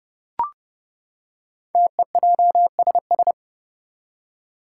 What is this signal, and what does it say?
Morse code, heard as a keyed tone, about 30 wpm, 700 Hz. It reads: TEJSS
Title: Code Groups - 5 Letters - 30wpm